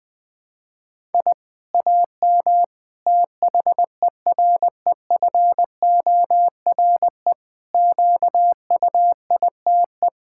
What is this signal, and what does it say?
Morse code, heard as a keyed tone, about 21 wpm, 700 Hz. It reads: I AM THEREFORE QUITE
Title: Sentences from Top 1000 Words - Review Part 6 - 20wpm